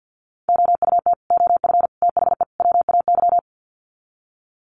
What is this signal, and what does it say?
Morse code, heard as a keyed tone, about 48 wpm, 700 Hz. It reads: OUT OF THE WAY
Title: Sets of 4 Words 50wpm